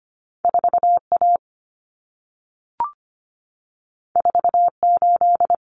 Morse code, heard as a keyed tone, about 25 wpm, 700 Hz. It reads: 4A 48
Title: Common Cut Numbers - 2-Digit 25wpm